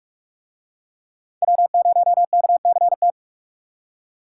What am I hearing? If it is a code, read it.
Morse code, heard as a keyed tone, about 40 wpm, 700 Hz. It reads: W0KCT